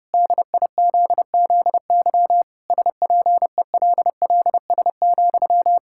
Morse code, heard as a keyed tone, about 30 wpm, 700 Hz. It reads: DIZZY SPELLS,
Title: Pangram Challenge v2 - 30wpm